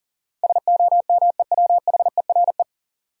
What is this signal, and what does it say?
Morse code, heard as a keyed tone, about 37 wpm, 700 Hz. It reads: SOMEWHERE
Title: Sentences from Top 1400 Words - New Word Drill Alt - 40wpm